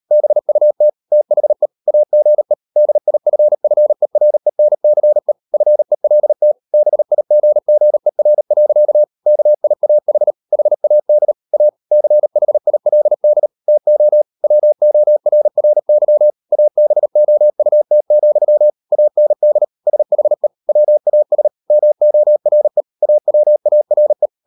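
Morse code, heard as a keyed tone, about 38 wpm, 600 Hz. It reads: BUT THE AGE DIFFERENCE FELT BIGGER. KIAH HAD A CHILD TO WORRY ABOUT, AND SHE WAS MORE AWARE
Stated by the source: Morse code practice at 38 WPM, sending Ken Follett's novel Never in English, from CW med Gnister.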